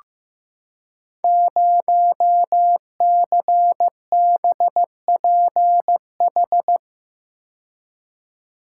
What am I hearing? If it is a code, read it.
Morse code, heard as a keyed tone, about 15 wpm, 700 Hz. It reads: 0CBPH
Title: Code Groups - 5 Letter:Numbers - 15wpm